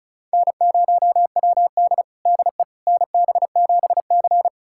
Morse code, heard as a keyed tone, about 34 wpm, 690 Hz. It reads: N0WD DE NB7C
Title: USA Callsign DE Callsign - 4 Characters - 35wpm